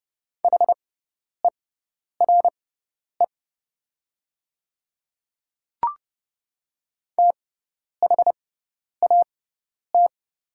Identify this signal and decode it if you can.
Morse code, heard as a keyed tone, about 32 wpm, spaced as for 10 wpm, 690 Hz. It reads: HERE THAT
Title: 6x Character Spacing - Top 500 Words - 30wpm